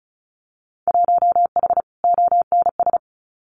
Morse code, heard as a keyed tone, about 34 wpm, 700 Hz. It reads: 1H ONS
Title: Winter Field Day Exchanges 35wpm